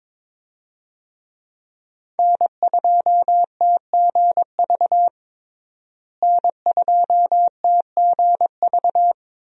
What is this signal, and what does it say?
Morse code, heard as a keyed tone, about 22 wpm, 700 Hz. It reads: N2TGV N2TGV
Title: USA Call Signs - 5 Characters - Repeated 3x - 22wpm